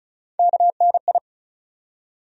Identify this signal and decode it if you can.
Morse code, heard as a keyed tone, about 34 wpm, 700 Hz. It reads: KNI
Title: USA Call Sign Postfixes - 3 Letters 35wpm